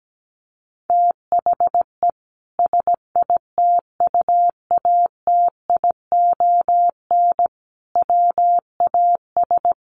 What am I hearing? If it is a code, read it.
Morse code, heard as a keyed tone, about 17 wpm, 700 Hz. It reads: THE SITUATION WAS
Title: Sentences from Top 1300 Words 17wpm